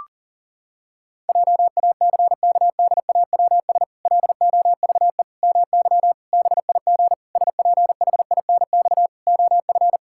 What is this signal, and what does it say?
Morse code, heard as a keyed tone, about 40 wpm, 700 Hz. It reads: JACKDAWS LOVE MY BIG SPHINX OF
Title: Pangram Challenge v2 - 40wpm